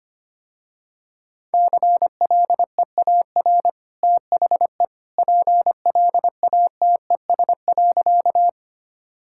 Morse code, heard as a keyed tone, about 25 wpm, 700 Hz. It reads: CLEAR THE PLATES.
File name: Easy Sentences with Hints - 25wpm